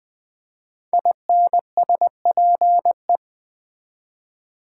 Morse code, heard as a keyed tone, about 20 wpm, 700 Hz. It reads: INSPE
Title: Build A Word - 5 Letters At A Time - 10-Letter Words - 20wpm